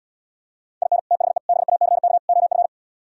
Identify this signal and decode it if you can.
Morse code, heard as a keyed tone, about 77 wpm, 700 Hz. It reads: IN THE BACK OF